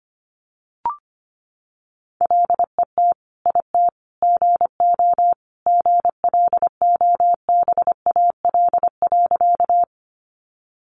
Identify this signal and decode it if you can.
Morse code, heard as a keyed tone, about 25 wpm, 700 Hz. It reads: LET IT GO GLOBAL.
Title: Sentences from Top 1200 Words 25wpm